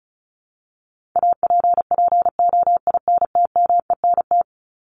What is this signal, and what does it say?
Morse code, heard as a keyed tone, about 34 wpm, 700 Hz. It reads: APPOINTMENT
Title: Top 1751-2000 Words 35wpm